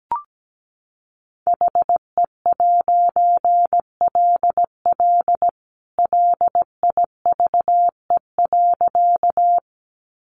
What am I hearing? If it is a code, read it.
Morse code, heard as a keyed tone, about 17 wpm, 700 Hz. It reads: HE'LL LIVE.
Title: A Sound Of Thunder 17wpm